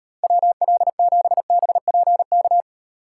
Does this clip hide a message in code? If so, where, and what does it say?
Yes: Morse code from 0.2 to 2.6 s, heard as a keyed tone, about 36 wpm, 700 Hz. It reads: WL7BPK